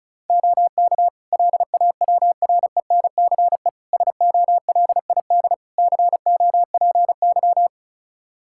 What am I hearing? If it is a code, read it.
Morse code, heard as a keyed tone, about 36 wpm, 700 Hz. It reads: OK LAWRENCE SOLID COPY
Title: Practice QSO - Exchange 3 - More Info - 35wpm